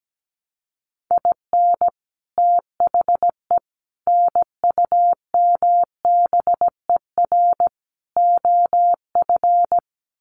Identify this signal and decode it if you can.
Morse code, heard as a keyed tone, about 17 wpm, 700 Hz. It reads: IN THE NUMBER OF
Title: Sets of 4 Words 17wpm